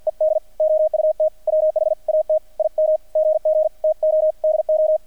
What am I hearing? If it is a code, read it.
Morse code, heard as a keyed tone, about 48 wpm, 620 Hz, with radio noise. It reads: ED OUT WHAT IM GG TO DO
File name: archived / CRSnet